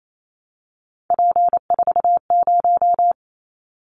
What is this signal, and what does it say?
Morse code, heard as a keyed tone, about 28 wpm, 700 Hz. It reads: P40
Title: Callsign Prefixes for Top 100 DXCC - 28wpm